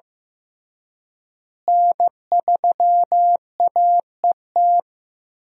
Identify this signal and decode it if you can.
Morse code, heard as a keyed tone, about 14 wpm, 700 Hz. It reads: N3AET
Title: USA Call Signs - 5 Characters - Repeated 2x - 15wpm